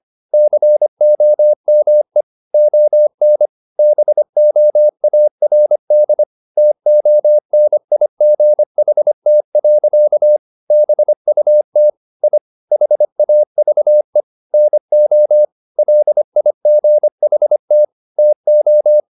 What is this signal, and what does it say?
Morse code, heard as a keyed tone, about 25 wpm, 600 Hz. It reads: COME ON BOARD TONIGHT. BUT I HAVE NO LIGHT TO
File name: the old man and the sea part 1 25wpm